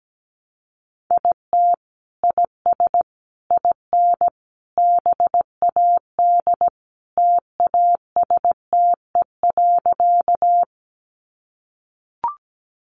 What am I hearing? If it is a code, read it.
Morse code, heard as a keyed tone, about 17 wpm, 700 Hz. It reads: IT IS IN BAD TASTE.
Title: Sentences from Top 1100 Words 17wpm